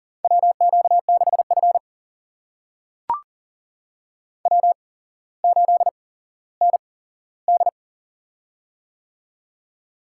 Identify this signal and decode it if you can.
Morse code, heard as a keyed tone, about 38 wpm, spaced as for 17 wpm, 700 Hz. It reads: WQ6F W8ND
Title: USA Call Signs - 8x Character Spacing - 40wpm